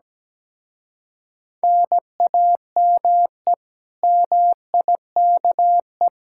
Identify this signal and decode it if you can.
Morse code, heard as a keyed tone, about 17 wpm, 700 Hz. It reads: NAME MIKE
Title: Speed-Racing Common QSO Phrases - 17wpm